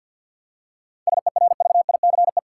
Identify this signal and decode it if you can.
Morse code, heard as a keyed tone, about 48 wpm, 700 Hz. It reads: SERVICE
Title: Practice QSO - Exchange 3 - More Info - 50wpm